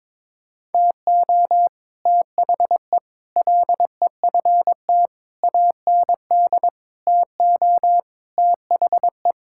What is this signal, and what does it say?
Morse code, heard as a keyed tone, about 22 wpm, 700 Hz. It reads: TO THE LEFT AND TO THE
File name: Sets of 9 Words - 22wpm